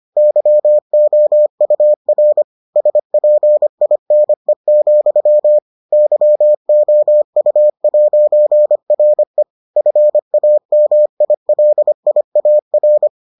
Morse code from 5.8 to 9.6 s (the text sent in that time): YOU'RE